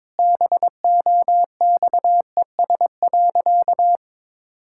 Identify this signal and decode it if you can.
Morse code, heard as a keyed tone, about 22 wpm, 700 Hz. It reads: BOXES.
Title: Pangram Challenge v2 - 22wpm